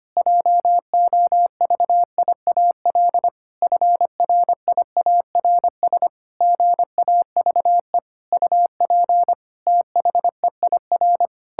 Morse code, heard as a keyed tone, about 25 wpm, 700 Hz. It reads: JOVIAL FRIARS GAVE UP THEIR
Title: Pangram Challenge v2 - 25wpm